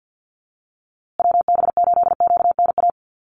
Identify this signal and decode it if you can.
Morse code, heard as a keyed tone, about 50 wpm, 700 Hz. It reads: WB8QNA
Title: USA Call Signs Repeated 3x 50wpm